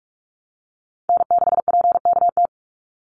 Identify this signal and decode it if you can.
Morse code, heard as a keyed tone, about 44 wpm, 700 Hz. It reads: N6PKT
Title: NA SOTA Chaser Callsigns 45wpm